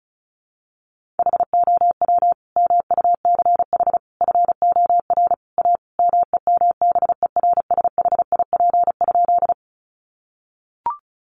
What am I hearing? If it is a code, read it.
Morse code, heard as a keyed tone, about 34 wpm, 700 Hz. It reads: HOW MUCH FOR A MEMBERSHIP?
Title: Sentences from Top 1400 Words 35wpm